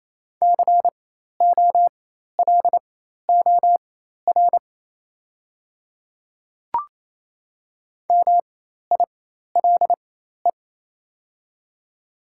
Morse code, heard as a keyed tone, about 27 wpm, spaced as for 13 wpm, 700 Hz. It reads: COLOR MILE
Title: 4x Character Spacing - Top 300 Words - 28wpm